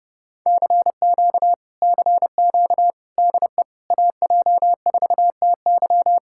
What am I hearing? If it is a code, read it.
Morse code, heard as a keyed tone, about 31 wpm, 700 Hz. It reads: CQ CQ DE AJ4TY